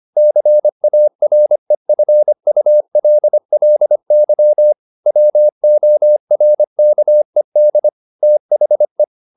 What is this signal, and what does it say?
Morse code, heard as a keyed tone, about 25 wpm, 600 Hz. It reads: CAREFULLY WORKED THE
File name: the old man and the sea part 1 25wpm